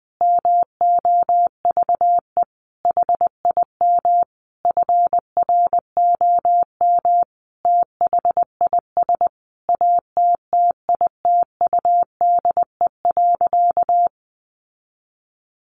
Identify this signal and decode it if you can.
Morse code, heard as a keyed tone, about 20 wpm, 700 Hz. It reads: MOVE HIM FROM THIS ATTITUDE.
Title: Sentences from Top 1300 Words 20wpm